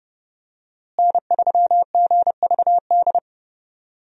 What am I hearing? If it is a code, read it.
Morse code, heard as a keyed tone, about 30 wpm, 700 Hz. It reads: N3GVD
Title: USA Call Signs - 5 Characters - Repeated 2x - 30wpm